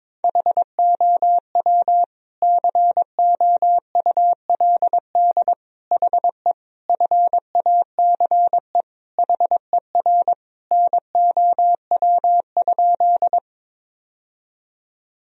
Morse code, heard as a keyed tone, about 22 wpm, 700 Hz. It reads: HOW COULD HE FACE HER NOW?
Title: Sentences from Top 1000 Words - Review Part 2 - 22wpm